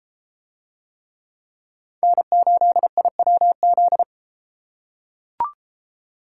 Morse code, heard as a keyed tone, about 29 wpm, 700 Hz. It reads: N8IWZ